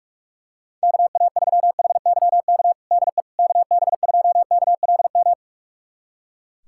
Morse code, heard as a keyed tone, about 41 wpm, 700 Hz. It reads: KA3HYK DE KB2KLM
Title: USA Callsign DE Callsign - 6 Characters - 45wpm